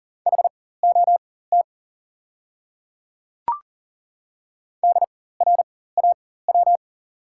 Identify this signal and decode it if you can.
Morse code, heard as a keyed tone, about 37 wpm, spaced as for 18 wpm, 690 Hz. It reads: HOT DRAW
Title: 4x Character Spacing - Top 500 Words - 40wpm